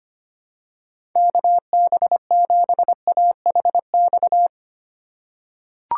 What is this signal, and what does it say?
Morse code, heard as a keyed tone, about 25 wpm, 700 Hz. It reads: KB7AHX
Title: USA Call Signs - 6 Characters - 2x Character Spacing - 25wpm